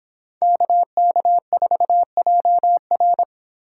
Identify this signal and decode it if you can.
Morse code, heard as a keyed tone, about 26 wpm, 700 Hz. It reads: KK4JR